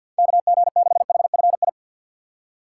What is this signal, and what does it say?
Morse code, heard as a keyed tone, about 54 wpm, 700 Hz. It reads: K G 6 H F I
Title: USA Call Signs - 6 Characters - 8x Character Spacing - 50wpm